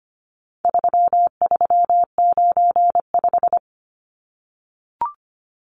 Morse code, heard as a keyed tone, about 25 wpm, 700 Hz. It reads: 3395
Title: Numbers - 4-Digit 25wpm